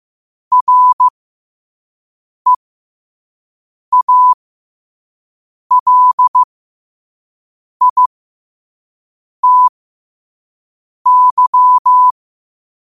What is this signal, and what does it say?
Morse code, heard as a keyed tone, about 15 wpm, spaced as for 5 wpm, 1000 Hz. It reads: REALITY
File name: 5wpm